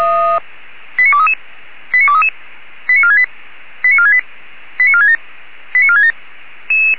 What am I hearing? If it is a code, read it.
Five-tone selective call (ZVEI), heard as quick runs of five tones, tones of about 70 ms. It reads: 89233 89233 89468 89468 89467 89467
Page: /464/SDR/ZVEI/89242/2023.06.21/